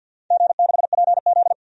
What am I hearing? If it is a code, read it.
Morse code, heard as a keyed tone, about 48 wpm, 700 Hz. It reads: G6PZ
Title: DXCC-100 Call Signs - 4 Characters - 50wpm